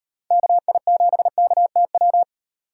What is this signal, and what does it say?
Morse code, heard as a keyed tone, about 36 wpm, 700 Hz. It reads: KI7KTW